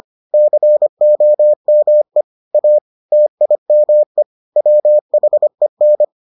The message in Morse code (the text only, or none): COME A TIME WHEN